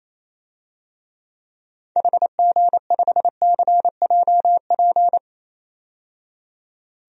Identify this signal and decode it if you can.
Morse code, heard as a keyed tone, about 27 wpm, 700 Hz. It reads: HG5CJP
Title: DXCC-100 Call Signs - 6 Characters - 28wpm